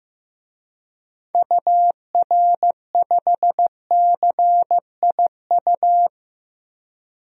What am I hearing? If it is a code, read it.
Morse code, heard as a keyed tone, about 15 wpm, 700 Hz. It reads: UR5CIU
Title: DXCC-100 Call Signs - 6 Characters - Repeated 3x - 15wpm